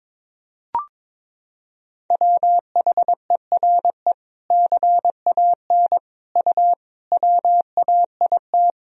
Morse code, heard as a keyed tone, about 22 wpm, 700 Hz. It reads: WHERE CAN U WAIT
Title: Sentences from Top 500 Words and Abbreviations 22wpm